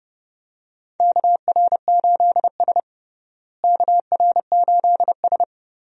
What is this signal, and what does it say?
Morse code, heard as a keyed tone, about 30 wpm, 700 Hz. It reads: KR8S KR8S
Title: USA Call Signs - 4 Characters - Repeated 3x - 30wpm